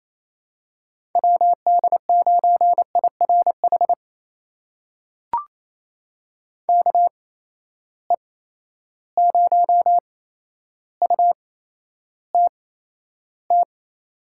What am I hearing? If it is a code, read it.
Morse code, heard as a keyed tone, about 28 wpm, spaced as for 12 wpm, 700 Hz. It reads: WD9IRH KE0UTT
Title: USA Call Signs - 6 Characters - 8x Character Spacing - 28wpm